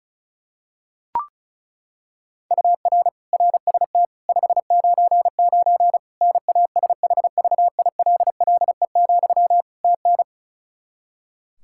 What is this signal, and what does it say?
Morse code, heard as a keyed tone, about 36 wpm, 700 Hz. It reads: UR RST 599 NASHVILLE, TN
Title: Practice QSO - Exchange 1 - Beginning - 35wpm